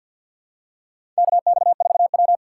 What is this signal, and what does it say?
Morse code, heard as a keyed tone, about 52 wpm, 700 Hz. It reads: KX4W
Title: USA Call Signs - 3x Character Spacing - 50wpm